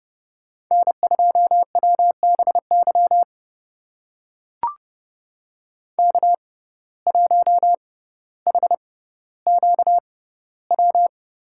Morse code, heard as a keyed tone, about 31 wpm, spaced as for 16 wpm, 700 Hz. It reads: N2WBY K1HQW
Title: USA Call Signs - 5 Characters - 6x Character Spacing - 30wpm